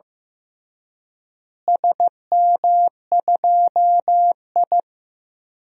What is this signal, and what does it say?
Morse code, heard as a keyed tone, about 15 wpm, 700 Hz. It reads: SM2I
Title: DXCC-100 Call Signs - 4 Characters - Repeated 3x - 15wpm